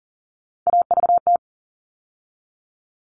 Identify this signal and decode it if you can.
Morse code, heard as a keyed tone, about 40 wpm, 700 Hz. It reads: AVT
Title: All Cut Numbers - 3-Digit 40wpm